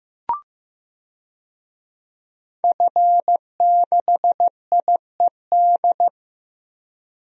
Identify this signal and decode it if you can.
Morse code, heard as a keyed tone, about 15 wpm, 700 Hz. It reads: F6IED
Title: DXCC-100 Call Signs - 5 Characters - Repeated 2x - 15wpm